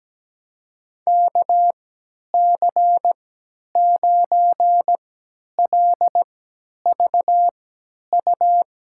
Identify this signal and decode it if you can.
Morse code, heard as a keyed tone, about 17 wpm, spaced as for 10 wpm, 700 Hz. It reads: KC9LVU
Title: USA Call Signs - 6 Characters - 3x Character Spacing - 17wpm